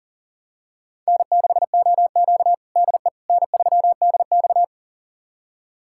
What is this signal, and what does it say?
Morse code, heard as a keyed tone, about 37 wpm, 700 Hz. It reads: N6OQ DE N3DX
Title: USA Callsign DE Callsign - 4 Characters - 40wpm